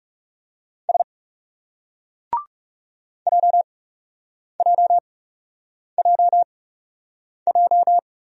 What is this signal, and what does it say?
Morse code, heard as a keyed tone, about 38 wpm, 700 Hz. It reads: S J J J J